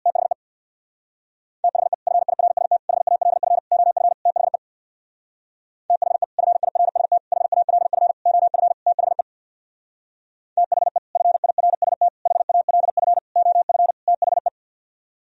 Morse code, heard as a keyed tone, about 60 wpm, 700 Hz. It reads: THE THE FIRST HALF OF THE THE FIRST HALF OF THE THE FIRST HALF OF THE
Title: Speed-Racing - Sets of 5 Words 50wpm